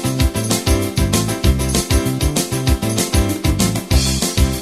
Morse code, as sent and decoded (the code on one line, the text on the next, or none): none